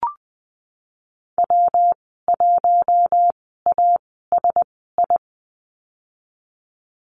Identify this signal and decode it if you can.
Morse code, heard as a keyed tone, about 20 wpm, spaced as for 15 wpm, 700 Hz. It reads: W1ASI
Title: 167 Lesson 27g - Number 7 - Call Signs 20wpm